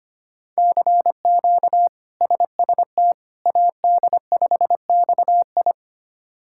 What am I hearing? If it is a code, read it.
Morse code, heard as a keyed tone, about 25 wpm, 700 Hz. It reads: CQ SST AD5XI
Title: K1USN SST QSO Practice - 25wpm